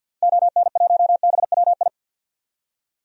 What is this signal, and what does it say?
Morse code, heard as a keyed tone, about 47 wpm, 700 Hz. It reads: ON1BWI
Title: DXCC-100 Call Signs - 6 Characters - 50wpm